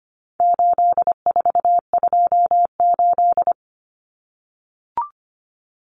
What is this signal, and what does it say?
Morse code, heard as a keyed tone, about 25 wpm, 700 Hz. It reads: 8428
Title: Numbers - 4-Digit 25wpm